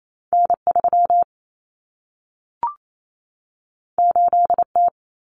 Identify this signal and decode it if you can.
Morse code, heard as a keyed tone, about 28 wpm, 700 Hz. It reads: N3 8T
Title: Common Cut Numbers - 2-Digit 28wpm